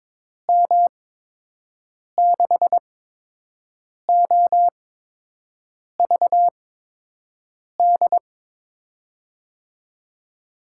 Morse code, heard as a keyed tone, about 22 wpm, spaced as for 6 wpm, 700 Hz. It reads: M6OVD
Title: USA Call Signs - 8x Character Spacing - 22wpm